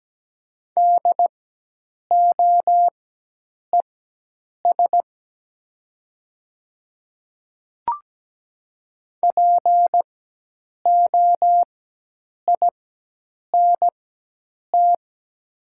Morse code, heard as a keyed tone, about 17 wpm, spaced as for 8 wpm, 700 Hz. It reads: DOES POINT
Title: 4x Character Spacing - Top 200 Words - 17wpm